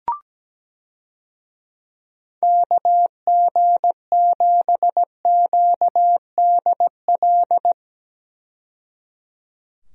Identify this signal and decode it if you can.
Morse code, heard as a keyed tone, about 17 wpm, 700 Hz. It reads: KG7QDL
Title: USA Call Signs - 6 Characters - 17wpm